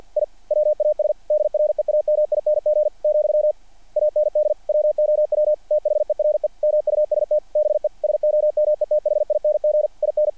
Morse code, heard as a keyed tone, about 48 wpm, 660 Hz. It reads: I WAS DREAMING , AND WOW THERE MUST BE SOMETHING IN
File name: archived / CRSnet